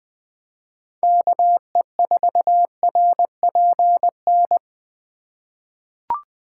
Morse code, heard as a keyed tone, about 20 wpm, 700 Hz. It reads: KE4RPN
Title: USA Call Signs - 6 Characters - 2x Character Spacing - 20wpm